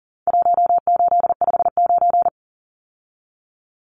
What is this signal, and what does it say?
Morse code, heard as a keyed tone, about 40 wpm, 700 Hz. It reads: 1859
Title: Numbers - 4-Digit 40wpm